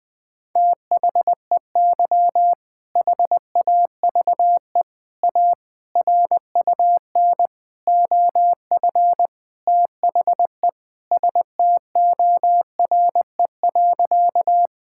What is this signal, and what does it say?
Morse code, heard as a keyed tone, about 20 wpm, 700 Hz. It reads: THEY HAVE A RUN OF THE STORE.
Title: Sentences from Top 1000 Words - Review Part 5 - 20wpm